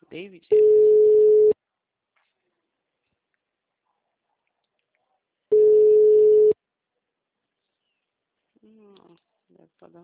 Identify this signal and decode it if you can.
Ringback tone (Europe, 425 Hz), with speech.